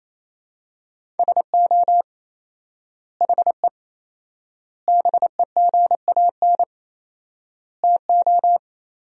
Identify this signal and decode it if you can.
Morse code, heard as a keyed tone, about 28 wpm, 700 Hz. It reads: SO HE BEGAN TO
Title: 4x Word-Spacing - Sentences from Top 300 Words - 28wpm